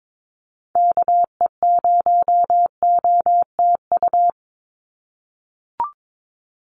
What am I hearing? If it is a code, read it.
Morse code, heard as a keyed tone, about 22 wpm, 700 Hz. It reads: KE0OTU
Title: USA Call Signs 22wpm